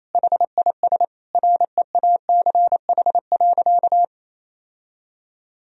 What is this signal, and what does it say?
Morse code, heard as a keyed tone, about 28 wpm, 700 Hz. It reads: HIS REACH.
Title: Sentences from Top 1000 Words - Review Part 4 - 28wpm